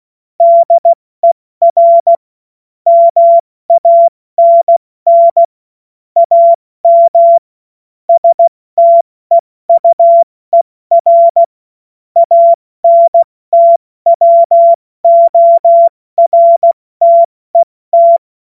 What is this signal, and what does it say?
Morse code, heard as a keyed tone, about 16 wpm, 680 Hz. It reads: DER MANN AM STEUER ANTWORTET
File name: Morsekurs / Witze / Witz 28. Dezember 2021, 70 F80